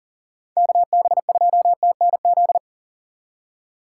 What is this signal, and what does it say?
Morse code, heard as a keyed tone, about 52 wpm, 700 Hz. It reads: KB2TNZ